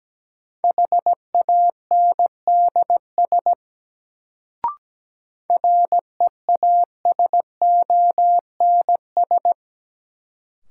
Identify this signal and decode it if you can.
Morse code, heard as a keyed tone, about 17 wpm, 700 Hz. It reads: HANDS REASONS
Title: Sets of 4 Words - New Word Drill - 17wpm